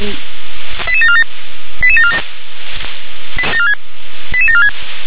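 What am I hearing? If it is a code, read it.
Five-tone selective call (ZVEI), heard as quick runs of five tones, tones of about 70 ms. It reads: 88747 88747 88746
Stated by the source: ZVEI five-tone selective call 88082 received with an SDR.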